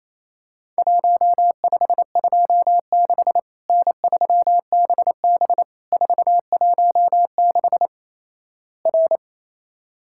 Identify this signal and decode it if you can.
Morse code, heard as a keyed tone, about 28 wpm, 700 Hz. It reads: 1526 N3BB 416 R
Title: Worked All Europe DX Contest - QTC Exchange - 28wpm